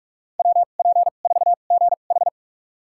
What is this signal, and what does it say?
Morse code, heard as a keyed tone, about 42 wpm, spaced as for 31 wpm, 690 Hz. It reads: WP4GH
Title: USA Call Signs - 5 Characters - 2x Character Spacing - 45wpm